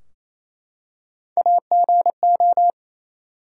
Morse code, heard as a keyed tone, about 28 wpm, 700 Hz. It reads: AGO
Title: Three Letter Words - 28wpm